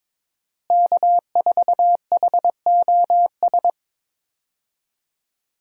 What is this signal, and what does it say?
Morse code, heard as a keyed tone, about 21 wpm, 700 Hz. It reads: K4HOS
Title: USA Call Signs - 5 Characters - Repeated 3x - 22wpm